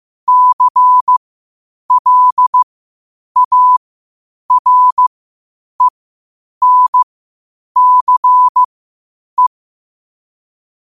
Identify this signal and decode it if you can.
Morse code, heard as a keyed tone, about 15 wpm, spaced as for 8 wpm, 1000 Hz. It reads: CLARENCE
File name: qotdfiles / 8wpm